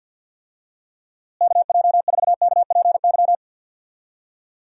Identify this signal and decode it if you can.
Morse code, heard as a keyed tone, about 47 wpm, 700 Hz. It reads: KJ4KPY
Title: USA Call Signs - 6 Characters - Repeated 2x - 50wpm